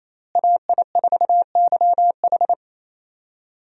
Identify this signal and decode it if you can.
Morse code, heard as a keyed tone, about 28 wpm, 700 Hz. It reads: AI4YH
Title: USA Call Signs - 5 Characters - Repeated 2x - 28wpm